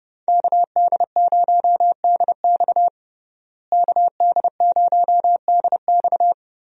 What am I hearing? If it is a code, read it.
Morse code, heard as a keyed tone, about 30 wpm, 700 Hz. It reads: KD0DX KD0DX
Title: USA Call Signs - 5 Characters - Repeated 2x - 30wpm